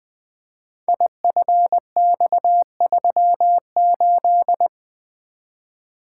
Morse code, heard as a keyed tone, about 20 wpm, 700 Hz. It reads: IFX38
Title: Code Groups - 5 Letter:Numbers - 20wpm